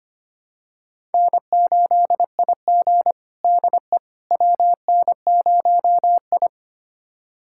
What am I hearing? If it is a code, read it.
Morse code, heard as a keyed tone, about 25 wpm, 700 Hz. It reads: N8IG DE WN0I
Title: USA Callsign DE Callsign - 4 Characters - 25wpm